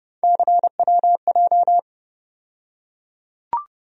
Morse code, heard as a keyed tone, about 30 wpm, 700 Hz. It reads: CWJ
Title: USA Call Sign Postfixes - 3 Letters 30wpm